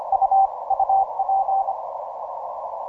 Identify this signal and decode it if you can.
Morse code, heard as a keyed tone, about 25 wpm, 780 Hz. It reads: U UP
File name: simple-file-list / JQ8NQJ JD1 17MCW